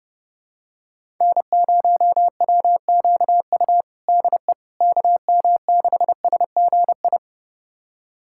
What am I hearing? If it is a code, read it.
Morse code, heard as a keyed tone, about 32 wpm, 700 Hz. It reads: N0WQU DE KM6SGI
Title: USA Callsign DE Callsign 30wpm